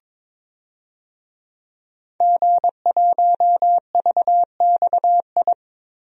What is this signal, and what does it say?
Morse code, heard as a keyed tone, about 22 wpm, 700 Hz. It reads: G1VXI
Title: DXCC-100 Call Signs - 5 Characters - Repeated 3x - 22wpm